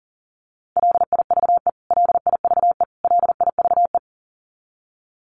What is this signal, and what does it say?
Morse code, heard as a keyed tone, about 40 wpm, 700 Hz. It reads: LIVE LIVE LIVE
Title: Top 500 Words Repeated 40wpm